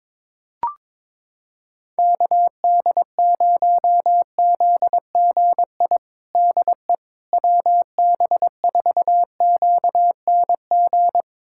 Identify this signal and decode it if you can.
Morse code, heard as a keyed tone, about 22 wpm, 700 Hz. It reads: KD0ZGI DE WB4QNG
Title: USA Callsign DE Callsign - 6 Characters - 22wpm